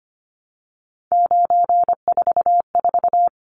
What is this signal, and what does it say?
Morse code, heard as a keyed tone, about 25 wpm, 700 Hz. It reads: 944
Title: Numbers - 3-Digit 25wpm